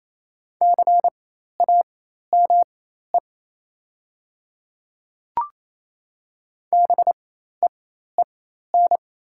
Morse code, heard as a keyed tone, about 28 wpm, spaced as for 13 wpm, 700 Hz. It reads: CAME BEEN
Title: 4x Character Spacing - Top 200 Words - 28wpm